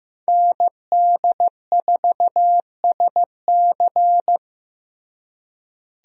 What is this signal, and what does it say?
Morse code, heard as a keyed tone, about 15 wpm, 700 Hz. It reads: ND4SC
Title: USA Call Signs - 5 Characters - Repeated 2x - 15wpm